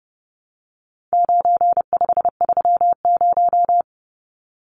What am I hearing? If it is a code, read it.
Morse code, heard as a keyed tone, about 30 wpm, 700 Hz. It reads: 9530
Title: Numbers - 4-Digit 30wpm